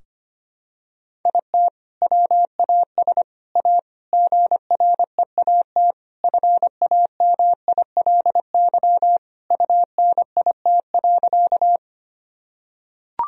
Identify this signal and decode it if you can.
Morse code, heard as a keyed tone, about 25 wpm, 700 Hz. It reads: IT WAS A GREAT FAMILY UNIT.
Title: Sentences from Top 1000 Words - Review Part 2 - 25wpm